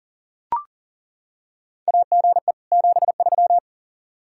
Morse code, heard as a keyed tone, about 38 wpm, 700 Hz. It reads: AGE 73
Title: K1USN SST QSO Practice 40wpm